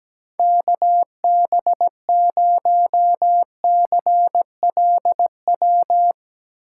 Morse code, heard as a keyed tone, about 17 wpm, 700 Hz. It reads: KB0CLW
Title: USA Call Signs - 6 Characters - 4x Character Spacing - 17wpm